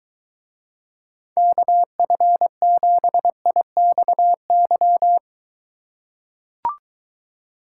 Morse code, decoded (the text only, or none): KF7IXY